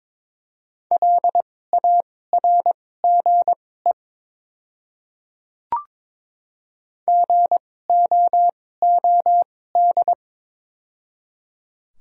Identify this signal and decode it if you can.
Morse code, heard as a keyed tone, about 22 wpm, spaced as for 16 wpm, 700 Hz. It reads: LARGE GOOD
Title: 2x Character Spacing - Top 300 Words - 22wpm